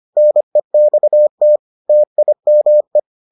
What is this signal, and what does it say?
Morse code, heard as a keyed tone, about 25 wpm, 600 Hz. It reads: NEXT TIME